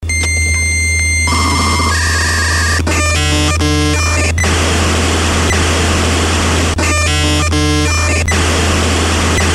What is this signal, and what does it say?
Dial-up modem handshake.